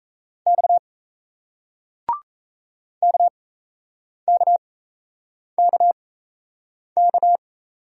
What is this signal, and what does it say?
Morse code, heard as a keyed tone, about 38 wpm, spaced as for 8 wpm, 700 Hz. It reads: X KKKK